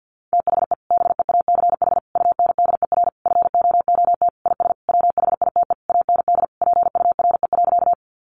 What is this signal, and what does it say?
Morse code, heard as a keyed tone, about 50 wpm, 700 Hz. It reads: THE BEACH UNDER FOOT IS WHITE AND PURE.
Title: Sentences from Top 1500 Words 50wpm